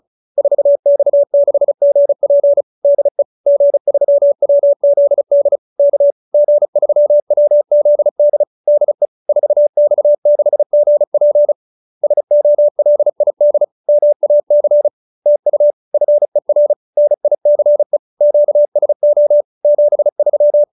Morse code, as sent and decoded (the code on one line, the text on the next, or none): ....- -..- -.... --. .--. / -.. / --. ...-- .-- --.. -.. / -.- / --. ...-- .-- --.. -.. / -.. / ....- -..- -.... --. .--. / ... --- .-.. .. -.. / -- .- -.-. / - ..- / ..-. .-. / -. .. -.-. / --.- ... --- / --... ...--
4X6GP DE G3WZD K G3WZD DE 4X6GP SOLID MAC TU FER NICE QSO 73